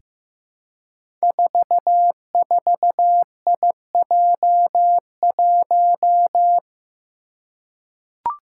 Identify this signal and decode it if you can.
Morse code, heard as a keyed tone, about 15 wpm, 700 Hz. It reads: 44IJ1
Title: Five Challenging Letter-Number Combinations 15wpm